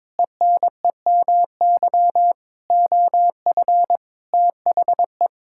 Morse code, heard as a keyed tone, about 22 wpm, 700 Hz. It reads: ENEMY OF THE
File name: Sets of 9 Words - 22wpm